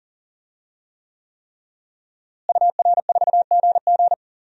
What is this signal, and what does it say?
Morse code, heard as a keyed tone, about 37 wpm, 700 Hz. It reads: UR4GG
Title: DXCC-100 Call Signs - 5 Characters - Repeated 3x - 40wpm